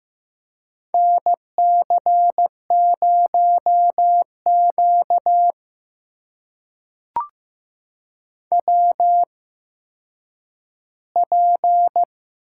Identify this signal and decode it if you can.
Morse code, heard as a keyed tone, about 15 wpm, spaced as for 9 wpm, 700 Hz. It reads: NC0Q WP
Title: USA Call Signs - 4 Characters - 8x Character Spacing - 15wpm